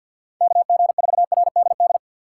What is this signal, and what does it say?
Morse code, heard as a keyed tone, about 52 wpm, 700 Hz. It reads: KG4RDD
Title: USA Call Signs - 6 Characters - 4x Character Spacing - 50wpm